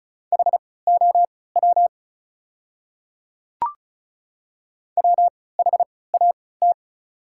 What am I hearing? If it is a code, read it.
Morse code, heard as a keyed tone, about 36 wpm, spaced as for 20 wpm, 700 Hz. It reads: HOW WHAT
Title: 3x Character Spacing - Top 300 Words - 35wpm